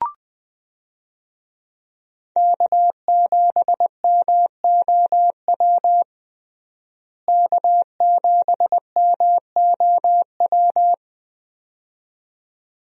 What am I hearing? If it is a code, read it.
Morse code, heard as a keyed tone, about 20 wpm, 700 Hz. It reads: K7MOW K7MOW
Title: USA Call Signs - 5 Characters - Repeated 2x - 20wpm